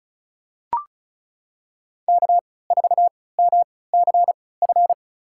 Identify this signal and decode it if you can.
Morse code, heard as a keyed tone, about 34 wpm, spaced as for 20 wpm, 700 Hz. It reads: K4MCF
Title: USA Call Signs - 5 Characters - 3x Character Spacing - 35wpm